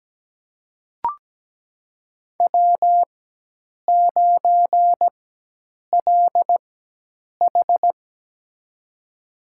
Morse code, heard as a keyed tone, about 17 wpm, spaced as for 8 wpm, 700 Hz. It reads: W9LH
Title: USA Call Signs - 4x Character Spacing - 17wpm